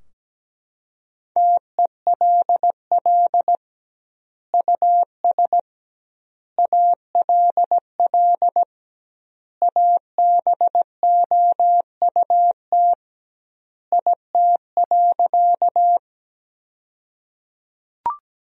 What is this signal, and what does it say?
Morse code, heard as a keyed tone, about 17 wpm, 700 Hz. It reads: TELL US ALL ABOUT IT.
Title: 2x Word-Spacing - Sentences from Top 200 Words - 17wpm